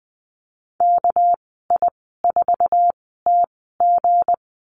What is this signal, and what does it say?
Morse code, heard as a keyed tone, about 20 wpm, spaced as for 15 wpm, 700 Hz. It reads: KI4TG
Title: 245 Lesson 39g - Letter z - Call Signs 20wpm